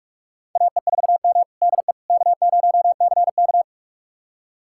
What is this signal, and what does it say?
Morse code, heard as a keyed tone, about 46 wpm, 700 Hz. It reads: AE4M DE K0CK
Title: USA Callsign DE Callsign - 4 Characters - 45wpm